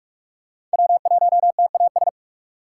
Morse code, heard as a keyed tone, about 39 wpm, 700 Hz. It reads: W1TAS